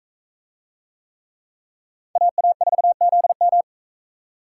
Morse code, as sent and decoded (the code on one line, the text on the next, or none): .- .- ....- --.. --
AA4ZM